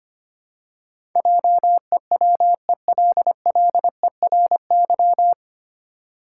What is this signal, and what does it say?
Morse code, heard as a keyed tone, about 25 wpm, 700 Hz. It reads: JEWELLERY
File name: Nine Letter Words - 25wpm